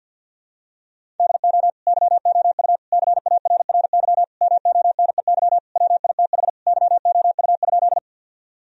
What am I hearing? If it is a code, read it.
Morse code, heard as a keyed tone, about 52 wpm, 700 Hz. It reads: DO YOU CARRY MONEY WITH YOU?
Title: Sentences from Top 1000 Words - Review Part 1 - 50wpm